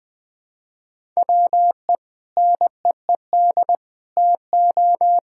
Morse code, heard as a keyed tone, about 21 wpm, 700 Hz. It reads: WE NEED TO
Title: Sentences with 4-Word Phrases - 20wpm